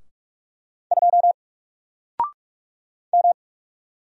Morse code, heard as a keyed tone, about 46 wpm, 700 Hz. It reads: 2 M
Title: Single Letter-Number 45wpm